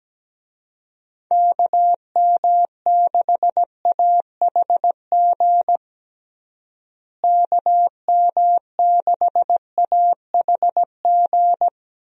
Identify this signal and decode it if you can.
Morse code, heard as a keyed tone, about 17 wpm, 700 Hz. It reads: KM6AHG KM6AHG
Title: USA Call Signs - 6 Characters - Repeated 2x - 17wpm